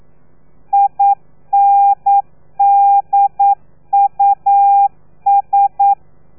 Morse code, heard as a keyed tone, about 9 wpm, 800 Hz. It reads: INDUS
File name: telegrafia / i2tfj / Lez3 40